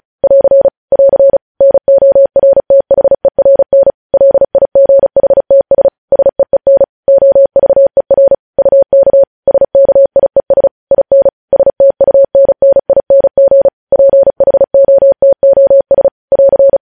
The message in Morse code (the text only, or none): <AR> <AR> NORTHERN LIGHTS SEEN OVER UK SKIES IN STUNNING PHOTOS <AR>